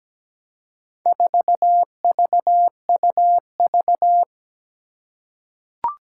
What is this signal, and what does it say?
Morse code, heard as a keyed tone, about 17 wpm, 700 Hz. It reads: 4VUV
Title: Four Challenging Letter-Number Combinations 17wpm